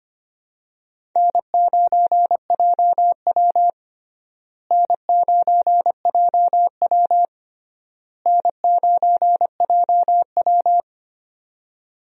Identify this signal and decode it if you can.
Morse code, heard as a keyed tone, about 25 wpm, 700 Hz. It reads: N9JW N9JW N9JW
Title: USA Call Signs - 4 Characters - Repeated 3x - 25wpm